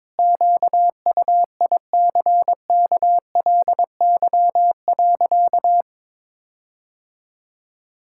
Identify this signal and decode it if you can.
Morse code, heard as a keyed tone, about 22 wpm, 700 Hz. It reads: QUICKLY.
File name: Sentences with 7-Word Phrases - 22wpm